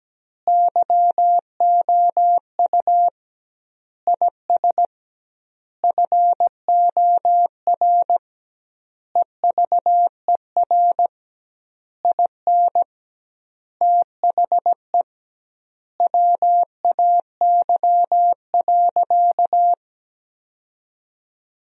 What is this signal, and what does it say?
Morse code, heard as a keyed tone, about 17 wpm, 700 Hz. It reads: YOU IS FOR EVER IN THE WAY.
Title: 2x Word-Spacing - Sentences from Top 400 Words - 17wpm